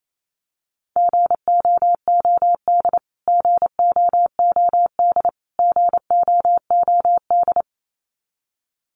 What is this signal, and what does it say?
Morse code, heard as a keyed tone, about 28 wpm, 700 Hz. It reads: GOOD GOOD GOOD
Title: Top 500 Words Repeated 28wpm